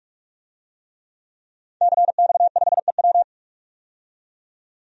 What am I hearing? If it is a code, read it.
Morse code, heard as a keyed tone, about 46 wpm, 700 Hz. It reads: CX5EW
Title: DXCC-100 Call Signs - 5 Characters - 45wpm